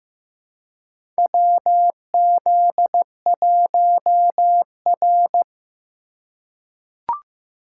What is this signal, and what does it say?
Morse code, heard as a keyed tone, about 15 wpm, 700 Hz. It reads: WZ1R
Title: USA Call Signs - 3x Character Spacing - 15wpm